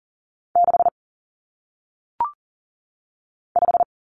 Morse code, heard as a keyed tone, about 40 wpm, 700 Hz. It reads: 6 5
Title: Numbers - 1-Digit 40wpm